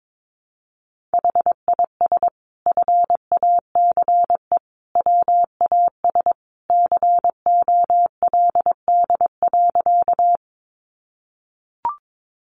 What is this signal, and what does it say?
Morse code, heard as a keyed tone, about 22 wpm, 700 Hz. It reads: HIS FACE WAS COLD.
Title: A Sound Of Thunder 22wpm